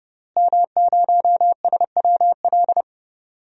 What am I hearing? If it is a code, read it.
Morse code, heard as a keyed tone, about 32 wpm, 700 Hz. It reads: M0SWL
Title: DXCC-100 Call Signs - 5 Characters - Repeated 3x - 30wpm